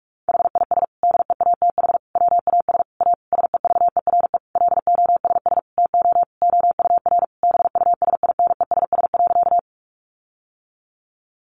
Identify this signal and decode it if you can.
Morse code, heard as a keyed tone, about 44 wpm, 700 Hz. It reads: HIS DEATH WAS A SEVERE LOSS TO OUR BUSINESS.
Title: Sentences from Top 1400 Words 45wpm